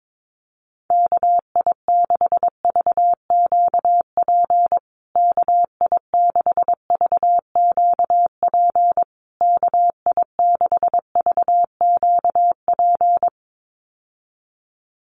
Morse code, heard as a keyed tone, about 22 wpm, 700 Hz. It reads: KI6VQP KI6VQP KI6VQP
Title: USA Call Signs Repeated 3x 22wpm